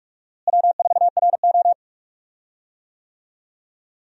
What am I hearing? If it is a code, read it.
Morse code, heard as a keyed tone, about 43 wpm, 700 Hz. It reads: W4RO
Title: USA Call Signs - 4 Characters - Repeated 2x - 45wpm